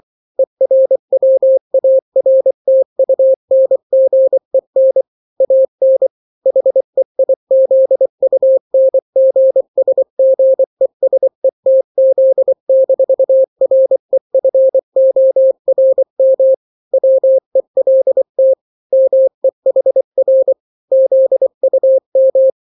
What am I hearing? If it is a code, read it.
Morse code, heard as a keyed tone, about 24 wpm, 530 Hz. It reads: ERWARTUNGEN AN HEIZUNGSGESETZ-REFORM WELT MEHR ZUM
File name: daily / 23wpm